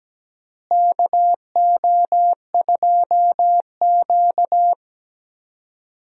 Morse code, heard as a keyed tone, about 17 wpm, 700 Hz. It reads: KO2Q
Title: USA Call Signs - 4 Characters - Repeated 2x - 17wpm